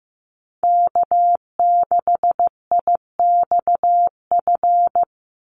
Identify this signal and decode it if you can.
Morse code, heard as a keyed tone, about 15 wpm, 700 Hz. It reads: K6IXF
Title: USA Call Signs 15wpm